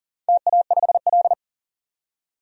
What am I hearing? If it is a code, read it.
Morse code, heard as a keyed tone, about 40 wpm, 700 Hz. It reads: TA5L
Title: DXCC-100 Call Signs - 4 Characters - 40wpm